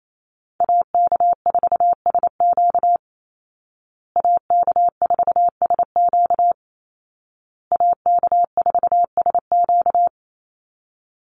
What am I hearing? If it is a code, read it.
Morse code, heard as a keyed tone, about 28 wpm, 700 Hz. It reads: AK4SQ AK4SQ AK4SQ
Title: Call Signs Repeated 2x - 28wpm